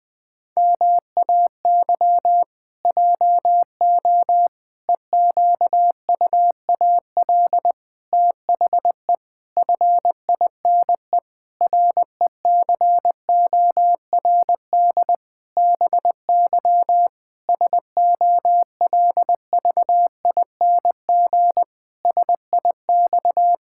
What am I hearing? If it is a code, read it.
Morse code, heard as a keyed tone, about 19 wpm, 700 Hz. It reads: MAY JO EQUAL THE FINE RECORD BY SOLVING SIX
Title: Pangram Challenge v2 - 20wpm